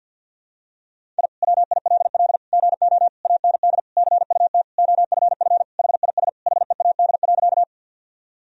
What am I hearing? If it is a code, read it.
Morse code, heard as a keyed tone, about 46 wpm, 700 Hz. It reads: I WILL GO AND CUT OFF HIS HEAD.
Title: Sentences from Top 1000 Words - Review Part 1 - 50wpm